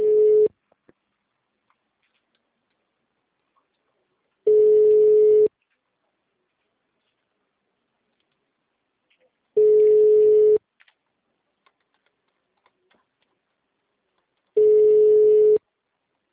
Ringback tone (Europe, 425 Hz).